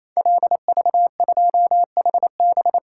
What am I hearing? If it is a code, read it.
Morse code, heard as a keyed tone, about 27 wpm, 700 Hz. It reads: LV2HB
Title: Code Groups - 5 Letter:Numbers - 28wpm